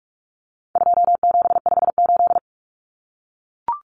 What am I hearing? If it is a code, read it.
Morse code, heard as a keyed tone, about 44 wpm, 700 Hz. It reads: 2758
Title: Numbers - 4-Digit 45wpm